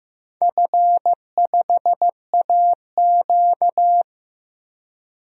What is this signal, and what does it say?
Morse code, heard as a keyed tone, about 15 wpm, 700 Hz. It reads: F5AQ
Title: DXCC-100 Call Signs - 4 Characters - Repeated 3x - 15wpm